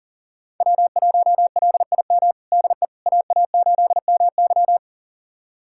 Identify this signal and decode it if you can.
Morse code, heard as a keyed tone, about 37 wpm, 700 Hz. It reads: W1LIM DE AA8MY
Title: USA Callsign DE Callsign - 5 Characters - 40wpm